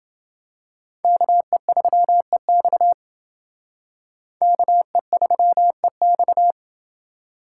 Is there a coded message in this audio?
Yes: Morse code from 1.0 to 6.5 s, heard as a keyed tone, about 28 wpm, 700 Hz. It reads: KE3EX KE3EX